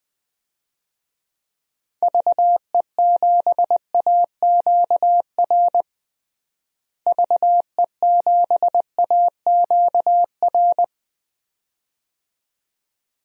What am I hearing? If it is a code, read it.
Morse code, heard as a keyed tone, about 20 wpm, 700 Hz. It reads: VE7AQR VE7AQR
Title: DXCC-100 Call Signs - 6 Characters - Repeated 2x - 20wpm